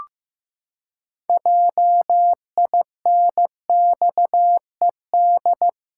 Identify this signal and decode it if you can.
Morse code, heard as a keyed tone, about 15 wpm, 700 Hz. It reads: JINXED
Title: Pangram Challenge v2 - 15wpm